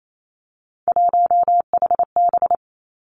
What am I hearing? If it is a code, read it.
Morse code, heard as a keyed tone, about 28 wpm, 700 Hz. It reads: 1HB
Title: Three Challenging Letter-Number Combinations 28wpm